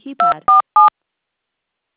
DTMF, heard as telephone touch tones, tones of about 140 ms. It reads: 247